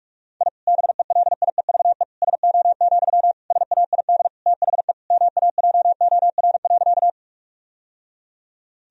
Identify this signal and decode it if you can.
Morse code, heard as a keyed tone, about 46 wpm, 700 Hz. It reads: I BELIEVE SO, SAID THE MAJOR.
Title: Sentences from Top 1000 Words - Review Part 4 - 45wpm